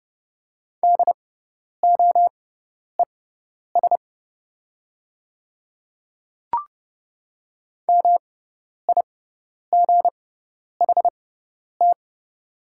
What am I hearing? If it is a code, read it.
Morse code, heard as a keyed tone, about 30 wpm, spaced as for 10 wpm, 700 Hz. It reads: DOES MIGHT
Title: 6x Character Spacing - Top 300 Words - 30wpm